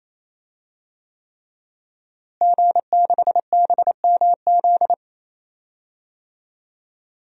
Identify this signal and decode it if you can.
Morse code, heard as a keyed tone, about 28 wpm, 700 Hz. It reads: G6BMZ
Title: DXCC-100 Call Signs - 5 Characters - Repeated 3x - 28wpm